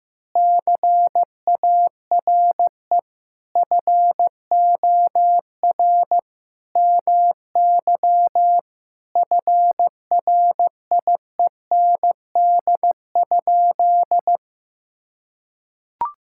Morse code, heard as a keyed tone, about 15 wpm, 700 Hz. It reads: CARE FOR MY FRIEND?
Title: Sentences from Top 1000 Words - Review Part 5 - 15wpm